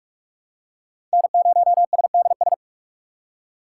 Morse code, heard as a keyed tone, about 44 wpm, 700 Hz. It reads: N0SDS
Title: USA Call Signs - 5 Characters - 2x Character Spacing - 45wpm